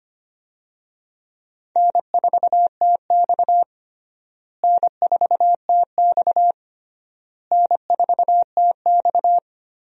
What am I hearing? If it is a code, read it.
Morse code, heard as a keyed tone, about 25 wpm, 700 Hz. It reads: N4TX N4TX N4TX
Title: USA Call Signs - 4 Characters - Repeated 3x - 25wpm